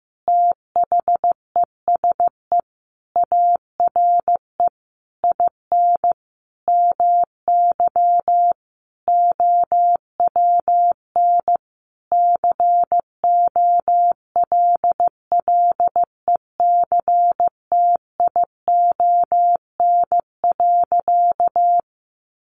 Morse code, heard as a keyed tone, about 15 wpm, 700 Hz. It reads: THESE ARE IN MY OWN COLLECTION.
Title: Sentences from Top 1200 Words 15wpm